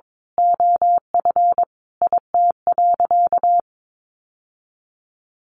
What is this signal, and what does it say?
Morse code, heard as a keyed tone, about 22 wpm, 700 Hz. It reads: OF IT.
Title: Sentences from Top 1200 Words 22wpm